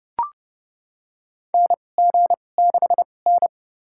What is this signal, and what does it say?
Morse code, heard as a keyed tone, about 32 wpm, spaced as for 22 wpm, 700 Hz. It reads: NG6N
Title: USA Call Signs - 4 Characters - 2x Character Spacing - 30wpm